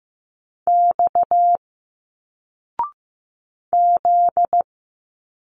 Morse code, heard as a keyed tone, about 15 wpm, 700 Hz. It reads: X Z
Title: Single Letters 15wpm